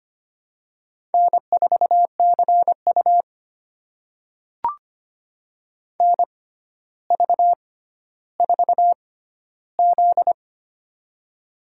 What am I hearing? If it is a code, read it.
Morse code, heard as a keyed tone, about 25 wpm, spaced as for 13 wpm, 700 Hz. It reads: N4CU NV4Z
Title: USA Call Signs - 4 Characters - 6x Character Spacing - 25wpm